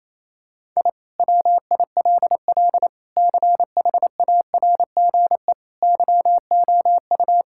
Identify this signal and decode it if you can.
Morse code, heard as a keyed tone, about 28 wpm, 700 Hz. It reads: I WILL CHARGE YOU
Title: Sentences from Top 1000 Words - Review Part 6 - 28wpm